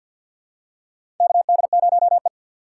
Morse code, heard as a keyed tone, about 48 wpm, 700 Hz. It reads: KD0E
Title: USA Call Signs - 6x Character Spacing - 50wpm